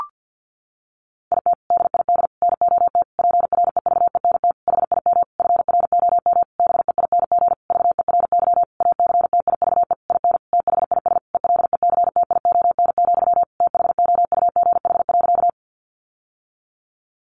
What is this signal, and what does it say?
Morse code, heard as a keyed tone, about 50 wpm, 700 Hz. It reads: IT DID NOT PREVENT HIM FROM BEING VERY ACTIVE IN THIS ELECTION, THOUGH.
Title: Sentences from Top 1500 Words 50wpm